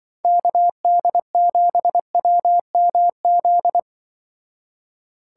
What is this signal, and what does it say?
Morse code, heard as a keyed tone, about 23 wpm, 700 Hz. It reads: KD7WMZ
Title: Speed Racing - USA 2x3 Call Signs 20wpm